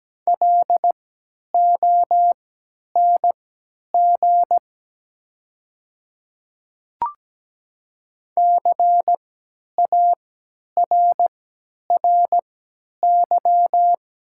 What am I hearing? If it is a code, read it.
Morse code, heard as a keyed tone, about 17 wpm, spaced as for 10 wpm, 700 Hz. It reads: LONG CARRY
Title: 3x Character Spacing - Top 300 Words - 17wpm